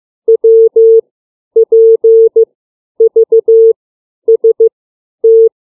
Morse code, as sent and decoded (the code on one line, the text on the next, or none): .-- / .--. / ...- / ... / -
W P V S T